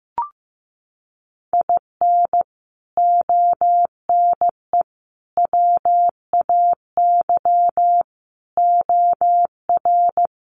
Morse code, heard as a keyed tone, about 15 wpm, 700 Hz. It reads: IN ONE WAY OR
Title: Sets of 5 Words Encore 15wpm